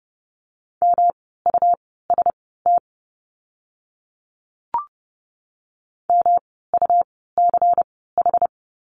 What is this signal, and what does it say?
Morse code, heard as a keyed tone, about 30 wpm, spaced as for 17 wpm, 700 Hz. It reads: MUST MUCH
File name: course / 30:10wpm / 237 Lesson 38f - Letter q - Words 30wpm